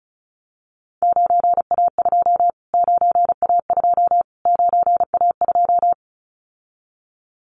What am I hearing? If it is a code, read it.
Morse code, heard as a keyed tone, about 34 wpm, 700 Hz. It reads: 9A2 9A2 9A2
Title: Callsign Prefixes for Top 25 DXCC - Repeated 3x - 35wpm